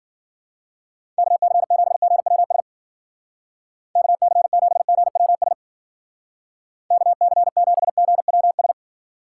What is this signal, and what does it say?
Morse code, heard as a keyed tone, about 48 wpm, 700 Hz. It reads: KC7GWS KC7GWS KC7GWS
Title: Speed Racing - USA 2x3 Call Signs 40wpm